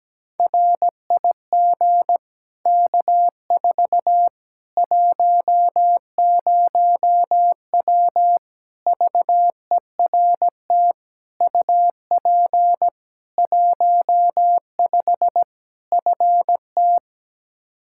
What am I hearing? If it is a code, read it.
Morse code, heard as a keyed tone, about 17 wpm, 700 Hz. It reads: RIG K4 10W VERT UP 15 FT
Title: Practice QSO - Exchange 2 - Info - 17wpm